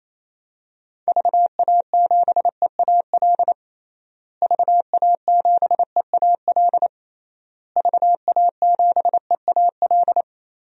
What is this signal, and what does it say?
Morse code, heard as a keyed tone, about 28 wpm, 700 Hz. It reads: VA7EAL VA7EAL VA7EAL
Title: DXCC-100 Call Signs - 6 Characters - Repeated 3x - 28wpm